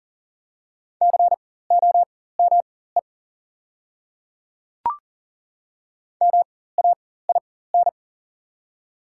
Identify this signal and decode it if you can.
Morse code, heard as a keyed tone, about 38 wpm, spaced as for 18 wpm, 690 Hz. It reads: COME MAIN
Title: 4x Character Spacing - Top 300 Words - 40wpm